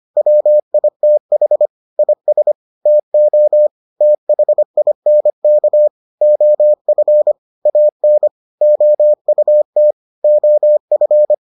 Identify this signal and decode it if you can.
Morse code, heard as a keyed tone, about 25 wpm, 600 Hz. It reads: WITH IS TO THINK OF AN OUT OF